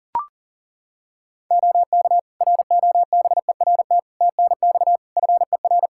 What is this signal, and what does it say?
Morse code, heard as a keyed tone, about 40 wpm, 700 Hz. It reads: OK ROBERT TNX FER
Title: Practice QSO - Exchange 4 - Ending - 40wpm